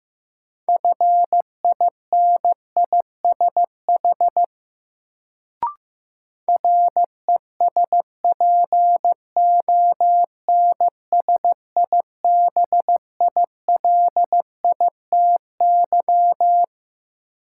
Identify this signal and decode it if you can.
Morse code, heard as a keyed tone, about 16 wpm, 700 Hz. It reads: FINISH RESPONSIBILITY
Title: Sentences from Top 1200 Words - New Word Drill Alt - 15wpm